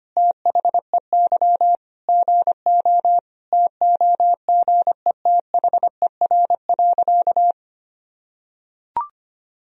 Morse code, heard as a keyed tone, about 25 wpm, 700 Hz. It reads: THEY GO TOGETHER.
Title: Sentences from Top 1000 Words - Review Part 3 - 25wpm